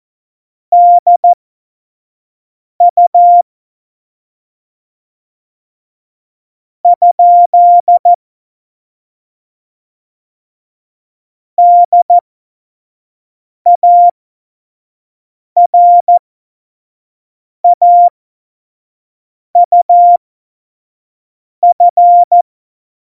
Morse code, heard as a keyed tone, about 14 wpm, spaced as for 5 wpm, 700 Hz. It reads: DU ? DARAUF
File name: Morsekurs / Witze / Witz 10 April 2018, 25 F70